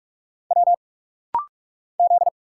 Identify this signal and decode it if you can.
Morse code, heard as a keyed tone, about 46 wpm, 700 Hz. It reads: W Z
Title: Single Letter-Number Alt - Rapid-Fire 45wpm